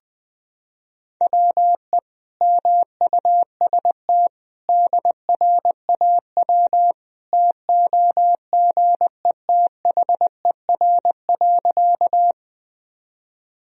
Morse code, heard as a keyed tone, about 20 wpm, 700 Hz. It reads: WE MUST DRAW TOGETHER.
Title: Sentences from Top 1000 Words - Review Part 1 - 20wpm